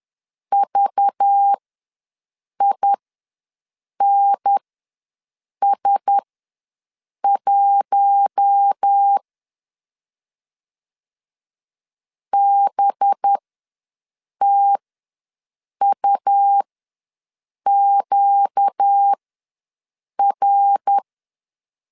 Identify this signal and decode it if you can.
Morse code, heard as a keyed tone, about 10 wpm, spaced as for 6 wpm, 800 Hz. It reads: VINS1 BTUQR
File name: LEZIONI / 5 LBVC / 5 LBVC ws 10wpm 3733